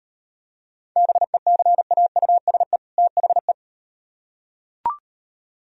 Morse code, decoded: BECAUSE THE

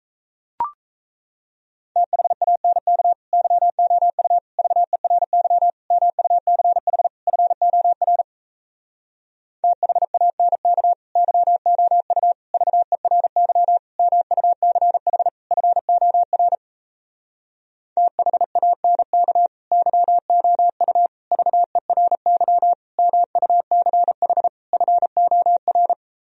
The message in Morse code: THANK YOU VERY MUCH FOR THANK YOU VERY MUCH FOR THANK YOU VERY MUCH FOR

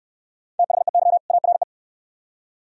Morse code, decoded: THEY ARE

